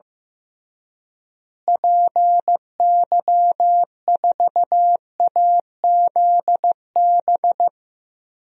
Morse code, heard as a keyed tone, about 15 wpm, 700 Hz. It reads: PY4AZB
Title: DXCC-100 Call Signs - 6 Characters - Repeated 3x - 15wpm